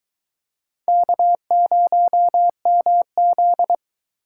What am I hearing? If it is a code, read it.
Morse code, heard as a keyed tone, about 23 wpm, 700 Hz. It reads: K0MZ